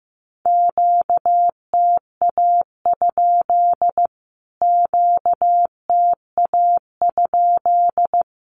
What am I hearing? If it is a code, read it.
Morse code, heard as a keyed tone, about 15 wpm, 700 Hz. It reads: QTA? QTA?
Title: Q-Codes Repeated 3x 15wpm